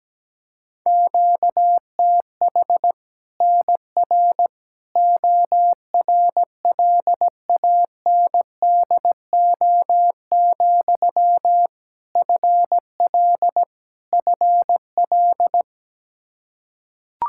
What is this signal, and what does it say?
Morse code, heard as a keyed tone, about 17 wpm, 700 Hz. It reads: QTH NR ORLANDO, FL FL
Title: Practice QSO - Exchange 1 - Beginning - 17wpm